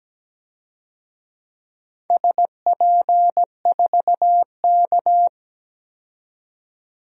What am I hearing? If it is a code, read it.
Morse code, heard as a keyed tone, about 17 wpm, 700 Hz. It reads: SP4K
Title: DXCC-100 Call Signs - 4 Characters - 17wpm